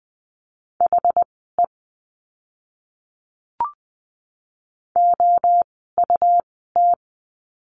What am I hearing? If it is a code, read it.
Morse code, heard as a keyed tone, about 20 wpm, spaced as for 15 wpm, 700 Hz. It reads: HE OUT
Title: 244 Lesson 39f - Letter z - Words 20wpm